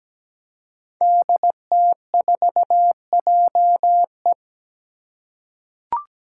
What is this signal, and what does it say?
Morse code, heard as a keyed tone, about 17 wpm, 700 Hz. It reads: DT4JE
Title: Five Challenging Letter-Number Combinations 17wpm